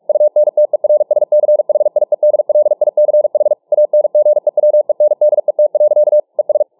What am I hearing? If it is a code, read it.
Morse code, heard as a keyed tone, about 44 wpm, 600 Hz. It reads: UNTERSCHIEDLICH ANGEWENDET. ES